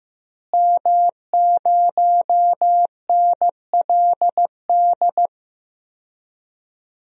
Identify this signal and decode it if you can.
Morse code, heard as a keyed tone, about 15 wpm, 700 Hz. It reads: M0NLD
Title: DXCC-100 Call Signs - 5 Characters - Repeated 2x - 15wpm